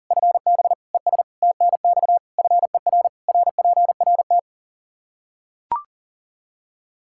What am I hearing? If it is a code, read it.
Morse code, heard as a keyed tone, about 37 wpm, 700 Hz. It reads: FB ES TNX FER RPRT
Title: Practice QSO - Exchange 2 - Info - 40wpm